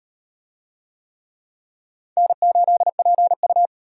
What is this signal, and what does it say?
Morse code, heard as a keyed tone, about 35 wpm, 700 Hz. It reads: N8PU